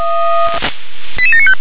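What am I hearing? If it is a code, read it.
Five-tone selective call (ZVEI), heard as quick runs of five tones, tones of about 70 ms. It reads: 88795
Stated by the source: SDR-received ZVEI five-tone call, code 88761.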